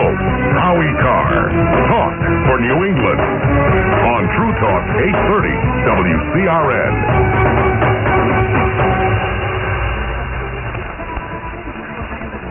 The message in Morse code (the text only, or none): none